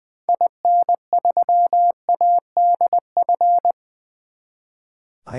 Morse code, heard as a keyed tone, about 20 wpm, 700 Hz. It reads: IN3ADF
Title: 2023 SOTA Activator Call Signs - 20wpm